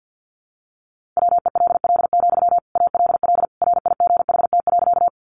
Morse code, heard as a keyed tone, about 50 wpm, 700 Hz. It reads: WELL, ALL RIGHT.
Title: Sentences from Top 200 Words 50wpm